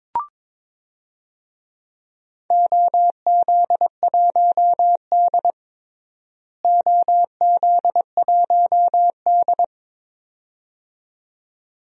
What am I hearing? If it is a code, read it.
Morse code, heard as a keyed tone, about 22 wpm, 700 Hz. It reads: OZ1D OZ1D
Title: DXCC-100 Call Signs - 4 Characters - Repeated 2x - 22wpm